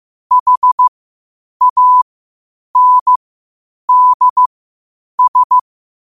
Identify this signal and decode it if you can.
Morse code, heard as a keyed tone, about 15 wpm, spaced as for 8 wpm, 1000 Hz. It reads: HANDS
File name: qotdfiles / 8wpm